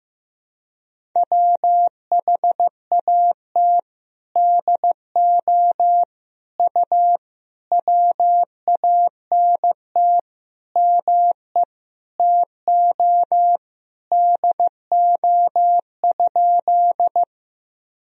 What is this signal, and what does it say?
Morse code, heard as a keyed tone, about 15 wpm, 700 Hz. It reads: WHAT DO U WANT ME TO DO?
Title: Sentences from Top 200 Words and Abbreviations 15wpm